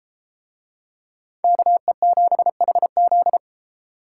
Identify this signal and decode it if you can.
Morse code, heard as a keyed tone, about 31 wpm, 700 Hz. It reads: KE7HZ